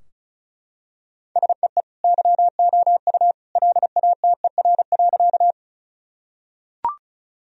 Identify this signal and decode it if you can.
Morse code, heard as a keyed tone, about 36 wpm, 700 Hz. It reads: SEE YOU LATER.
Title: Sentences from Top 1000 Words - Review Part 4 - 35wpm